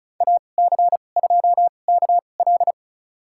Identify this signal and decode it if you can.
Morse code, heard as a keyed tone, about 36 wpm, spaced as for 25 wpm, 700 Hz. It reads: AC2KL
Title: USA Call Signs - 2x Character Spacing - 35wpm